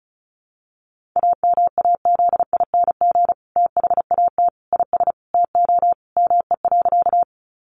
Morse code, heard as a keyed tone, about 34 wpm, 700 Hz. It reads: AMAZING THAT IS TO ME.
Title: Sentences from Top 1500 Words 35wpm